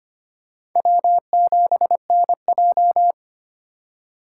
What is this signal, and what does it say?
Morse code, heard as a keyed tone, about 25 wpm, 700 Hz. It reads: W7NJ
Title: USA Call Signs - 3x Character Spacing - 25wpm